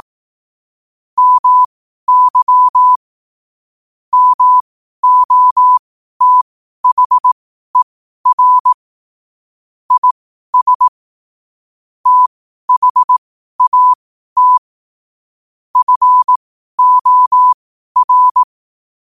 Morse code, heard as a keyed tone, about 18 wpm, spaced as for 12 wpm, 1000 Hz. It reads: MY MOTHER IS THAT FOR